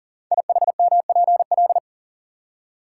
Morse code, heard as a keyed tone, about 42 wpm, 700 Hz. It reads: IHMPL
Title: Code Groups - 5 Letters - 40wpm